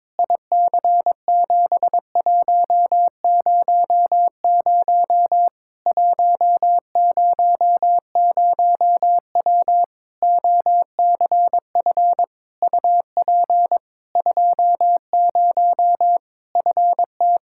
Morse code, heard as a keyed tone, about 22 wpm, 700 Hz. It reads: IC7100 100W OCF UP 20 FT
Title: Practice QSO - Exchange 2 - Info - 22wpm